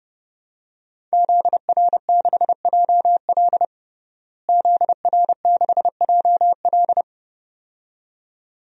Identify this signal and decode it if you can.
Morse code, heard as a keyed tone, about 30 wpm, 700 Hz. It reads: ZR6JL ZR6JL
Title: DXCC-100 Call Signs - 5 Characters - Repeated 2x - 30wpm